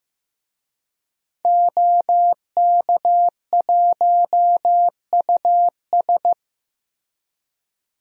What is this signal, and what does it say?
Morse code, heard as a keyed tone, about 16 wpm, 700 Hz. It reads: OK1US
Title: DXCC-100 Call Signs - 5 Characters - Repeated 3x - 15wpm